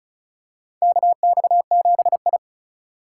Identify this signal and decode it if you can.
Morse code, heard as a keyed tone, about 36 wpm, 700 Hz. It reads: KX7I
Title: USA Call Signs - 4 Characters - 6x Character Spacing - 35wpm